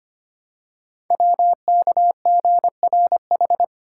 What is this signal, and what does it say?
Morse code, heard as a keyed tone, about 26 wpm, 700 Hz. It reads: WKGRH
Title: Code Groups - 5 Letters - 25wpm